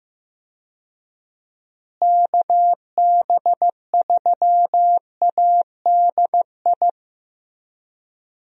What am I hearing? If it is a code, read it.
Morse code, heard as a keyed tone, about 15 wpm, 700 Hz. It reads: KB3ADI
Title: USA Call Signs - 6 Characters - Repeated 2x - 15wpm